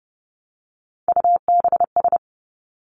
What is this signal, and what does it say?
Morse code, heard as a keyed tone, about 30 wpm, 700 Hz. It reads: UBS
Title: Three Challenging Letter-Number Combinations 30wpm